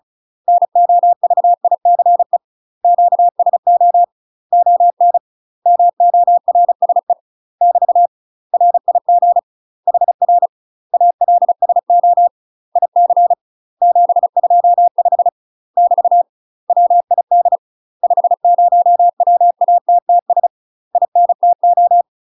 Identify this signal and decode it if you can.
Morse code, heard as a keyed tone, about 36 wpm, 700 Hz. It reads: NOVICE QSO ON MORSE <BT> RIG HR ALSO IC 725 <BT> WID 50 WATTS INTO
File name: UK MASTER QSOs TRAINING / 35WPM / UK MASTER QSOs 0041